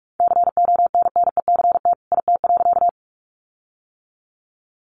Morse code, heard as a keyed tone, about 44 wpm, 700 Hz. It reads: CONNECT IT.
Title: Sentences from Top 1500 Words 45wpm